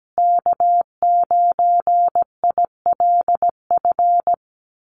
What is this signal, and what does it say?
Morse code, heard as a keyed tone, about 17 wpm, 700 Hz. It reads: K9ILF
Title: USA Call Signs Repeated 3x 17wpm